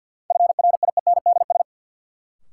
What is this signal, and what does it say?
Morse code, heard as a keyed tone, about 52 wpm, 700 Hz. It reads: FRIENDS
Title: Sentences from Top 1600 Words - New Word Drill Alt - 50wpm